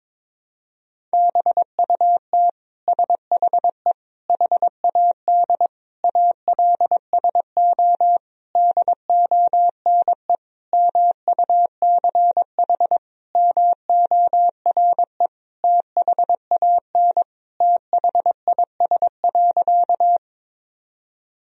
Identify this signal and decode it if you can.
Morse code, heard as a keyed tone, about 22 wpm, 700 Hz. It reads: BUT SHE HAD ALSO DONE MUCH MORE THAN THIS.
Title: Sentences from Top 1000 Words - Review Part 1 - 22wpm